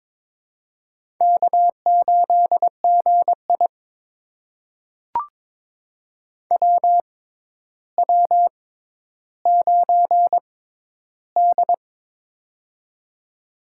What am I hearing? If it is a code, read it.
Morse code, heard as a keyed tone, about 22 wpm, spaced as for 11 wpm, 700 Hz. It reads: K8GI WW9D
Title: USA Call Signs - 4 Characters - 6x Character Spacing - 22wpm